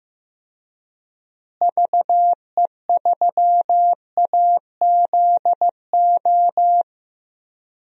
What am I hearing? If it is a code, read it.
Morse code, heard as a keyed tone, about 15 wpm, 700 Hz. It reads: VE3AZO
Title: DXCC-100 Call Signs - 6 Characters - Repeated 2x - 15wpm